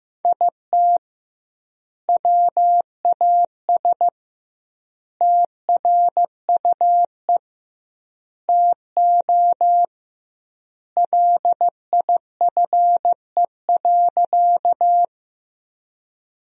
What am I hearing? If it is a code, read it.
Morse code, heard as a keyed tone, about 15 wpm, 700 Hz. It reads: IT WAS TRUE TO LIFE.
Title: 2x Word-Spacing - Sentences from Top 400 Words - 15wpm